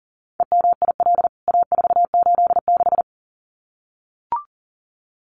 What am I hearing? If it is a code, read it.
Morse code, heard as a keyed tone, about 40 wpm, 700 Hz. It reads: EMIL A486
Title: CWT Exchange 40wpm